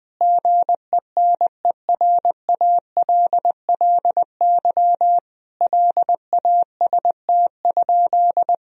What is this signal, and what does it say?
Morse code, heard as a keyed tone, about 20 wpm, 700 Hz. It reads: GENERALLY LAST?
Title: Sentences from Top 1000 Words - Review Part 4 - 20wpm